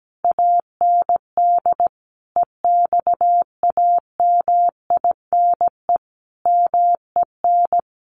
Morse code, heard as a keyed tone, about 17 wpm, 700 Hz. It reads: AND EXAMINE MEN
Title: A Sound Of Thunder 17wpm